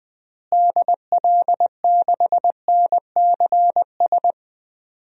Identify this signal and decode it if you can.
Morse code, heard as a keyed tone, about 21 wpm, 700 Hz. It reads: DL6NCS
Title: DXCC-100 Call Signs - 6 Characters - Repeated 2x - 20wpm